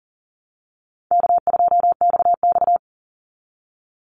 Morse code, heard as a keyed tone, about 40 wpm, 700 Hz. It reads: K2XX
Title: Easy Call Signs Repeated 2x - 40wpm